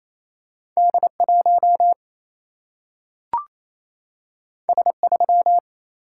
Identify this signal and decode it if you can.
Morse code, heard as a keyed tone, about 28 wpm, 700 Hz. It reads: D1 S3
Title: Single-Letter Single-Number 28wpm